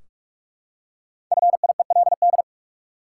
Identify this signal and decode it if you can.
Morse code, heard as a keyed tone, about 42 wpm, 700 Hz. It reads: FIELD
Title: Five Letter Words - 45wpm